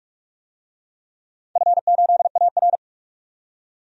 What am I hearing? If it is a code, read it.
Morse code, heard as a keyed tone, about 46 wpm, 700 Hz. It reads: F8AR
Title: DXCC-100 Call Signs - 4 Characters - 45wpm